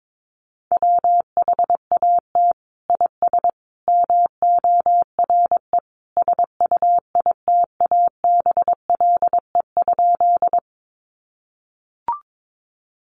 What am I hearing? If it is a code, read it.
Morse code, heard as a keyed tone, about 22 wpm, 700 Hz. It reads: WHAT IS MORE SUITABLE?
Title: Sentences from Top 1400 Words 22wpm